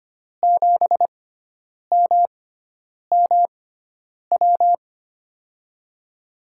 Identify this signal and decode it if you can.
Morse code, heard as a keyed tone, about 25 wpm, spaced as for 9 wpm, 700 Hz. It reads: 7MMW
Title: USA Call Signs - 6 Characters - 6x Character Spacing - 25wpm